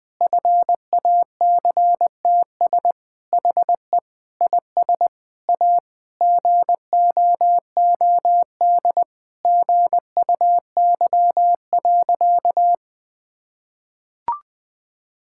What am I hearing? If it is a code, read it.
Morse code, heard as a keyed tone, about 20 wpm, 700 Hz. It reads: FACTS HE IS A GOOD GUY.
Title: Sentences with 3-Word Phrases - 20wpm